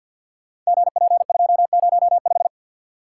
Morse code, heard as a keyed tone, about 47 wpm, 700 Hz. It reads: GP205
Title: Code Groups - 5 Letter:Numbers - 50wpm